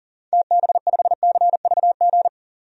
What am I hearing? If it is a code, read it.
Morse code, heard as a keyed tone, about 40 wpm, 700 Hz. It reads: TB5CVG
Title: DXCC-100 Call Signs - 6 Characters - 40wpm